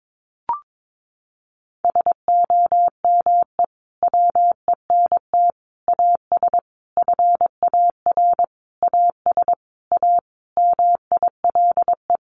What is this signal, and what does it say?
Morse code, heard as a keyed tone, about 22 wpm, 700 Hz. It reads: SOME WENT AS FAR AS A MILE
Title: Sentences from Top 1100 Words 22wpm